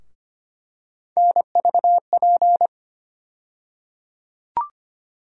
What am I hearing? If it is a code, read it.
Morse code, heard as a keyed tone, about 25 wpm, 700 Hz. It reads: NVP
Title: USA Call Sign Postfixes - 3 Letters 25wpm